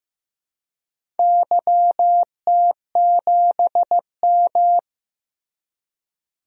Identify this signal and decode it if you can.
Morse code, heard as a keyed tone, about 15 wpm, 700 Hz. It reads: YT7M
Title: DXCC-100 Call Signs - 4 Characters - Repeated 3x - 15wpm